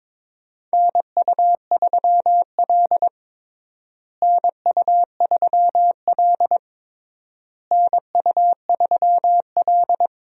Morse code, heard as a keyed tone, about 22 wpm, 700 Hz. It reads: NU3L NU3L NU3L
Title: USA Call Signs - 4 Characters - Repeated 3x - 22wpm